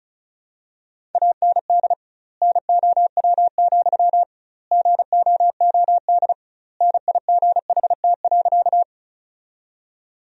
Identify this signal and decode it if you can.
Morse code, heard as a keyed tone, about 34 wpm, 700 Hz. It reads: AND NOW, GOOD NIGHT.
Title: 2x Word-Spacing - Sentences from Top 300 Words - 35wpm